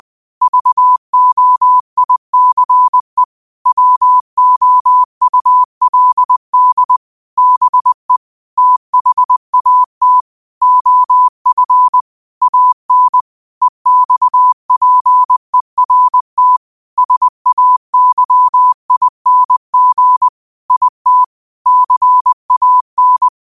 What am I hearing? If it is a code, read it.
Morse code, heard as a keyed tone, about 20 wpm, 1000 Hz. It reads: VOICE WOULD BE THAT OF AN EXPERT SAYING IT CAN